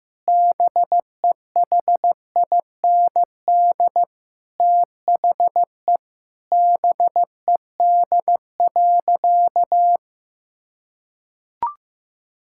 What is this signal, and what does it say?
Morse code, heard as a keyed tone, about 15 wpm, 700 Hz. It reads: BEHIND THE BED.
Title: Sentences from Top 1000 Words - Review Part 5 - 15wpm